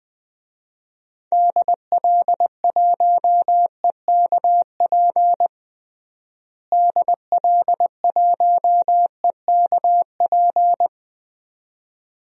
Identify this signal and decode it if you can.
Morse code, heard as a keyed tone, about 20 wpm, 700 Hz. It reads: DL1EKP DL1EKP
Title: DXCC-100 Call Signs - 6 Characters - Repeated 2x - 20wpm